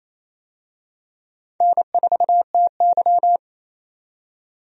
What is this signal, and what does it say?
Morse code, heard as a keyed tone, about 28 wpm, 700 Hz. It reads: N4TY
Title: USA Call Signs - 4 Characters - 28wpm